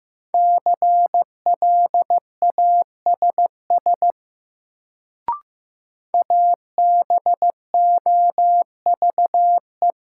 Morse code, heard as a keyed tone, about 15 wpm, 700 Hz. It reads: CLASS ABOVE
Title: Sentences from Top 1000 Words - Review Part 2 - New Word Drill - 15wpm